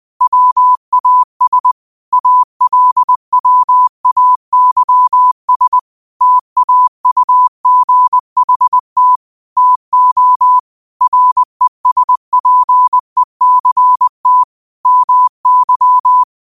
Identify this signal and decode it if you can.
Morse code, heard as a keyed tone, about 20 wpm, 1000 Hz. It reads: WAS ALWAYS TAUGHT TO RESPECT MY